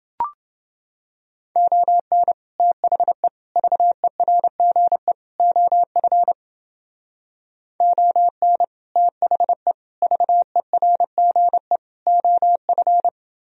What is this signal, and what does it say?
Morse code, heard as a keyed tone, about 28 wpm, 700 Hz. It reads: ON THE VERGE OF ON THE VERGE OF